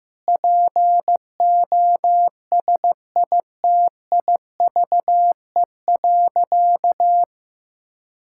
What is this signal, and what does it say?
Morse code, heard as a keyed tone, about 15 wpm, 700 Hz. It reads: POSITIVE.
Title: Sentences from Top 1000 Words - Review Part 5 - 15wpm